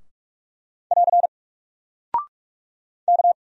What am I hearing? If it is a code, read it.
Morse code, heard as a keyed tone, about 44 wpm, 700 Hz. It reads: <AR> K
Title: Single Character (Letter:Number:Punctuation:Prosign) 45wpm